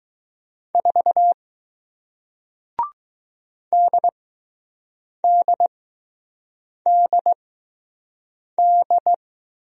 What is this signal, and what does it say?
Morse code, heard as a keyed tone, about 20 wpm, spaced as for 6 wpm, 700 Hz. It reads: 4 DDDD